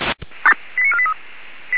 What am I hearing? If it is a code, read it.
Five-tone selective call (ZVEI), heard as quick runs of five tones, tones of about 70 ms. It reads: 89483